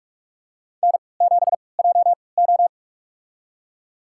Morse code, heard as a keyed tone, about 46 wpm, spaced as for 25 wpm, 700 Hz. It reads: N7JO
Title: USA Call Signs - 4 Characters - 3x Character Spacing - 45wpm